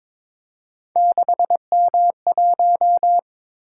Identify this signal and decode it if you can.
Morse code, heard as a keyed tone, about 22 wpm, 700 Hz. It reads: 6M1
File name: practice / 22wpm / Three Letter-Number Combinations - 22wpm